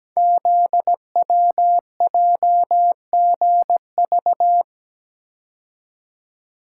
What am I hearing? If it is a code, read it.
Morse code, heard as a keyed tone, about 17 wpm, 700 Hz. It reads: ZWJGV
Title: Code Groups - 5 Letters - 17wpm